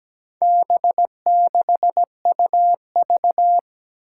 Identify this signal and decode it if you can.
Morse code, heard as a keyed tone, about 17 wpm, 700 Hz. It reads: B6UV
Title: Four Challenging Letter-Number Combinations 17wpm